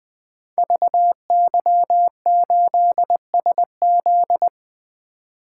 Morse code, heard as a keyed tone, about 20 wpm, 700 Hz. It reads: VY8SZ
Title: Code Groups - 5 Letter:Numbers - 20wpm